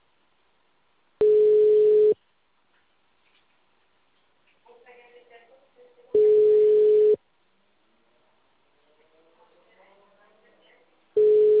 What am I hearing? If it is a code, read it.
Ringback tone (Europe, 425 Hz), with speech.
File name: out- -2004- - - .84546